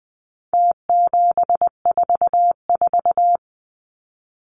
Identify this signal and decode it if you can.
Morse code, heard as a keyed tone, about 20 wpm, 700 Hz. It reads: T744
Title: Numbers - Contest Style - 4-Digit 20wpm